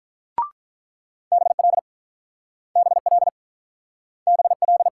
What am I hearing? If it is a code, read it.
Morse code, heard as a keyed tone, about 48 wpm, 700 Hz. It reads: BL BL BL